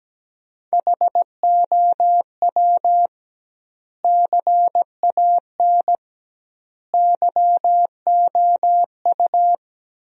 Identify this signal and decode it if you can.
Morse code, heard as a keyed tone, about 17 wpm, 700 Hz. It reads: HOW CAN YOU
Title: 2x Word-Spacing - Sentences from Top 100 Words - 17wpm